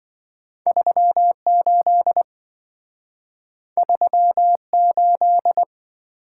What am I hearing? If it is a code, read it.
Morse code, heard as a keyed tone, about 27 wpm, 700 Hz. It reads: 38 38